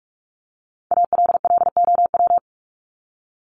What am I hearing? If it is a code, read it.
Morse code, heard as a keyed tone, about 44 wpm, 700 Hz. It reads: ALLOW
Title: Top 501-750 Words 45wpm